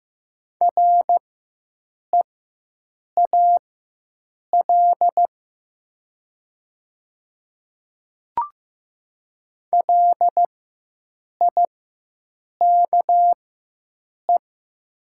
Morse code, heard as a keyed tone, about 15 wpm, spaced as for 7 wpm, 700 Hz. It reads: REAL LIKE
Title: 4x Character Spacing - Top 300 Words - 15wpm